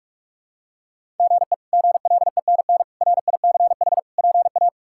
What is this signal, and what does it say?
Morse code, heard as a keyed tone, about 48 wpm, 700 Hz. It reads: GE GLENN RICH PA
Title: K1USN SST QSO Practice - 45wpm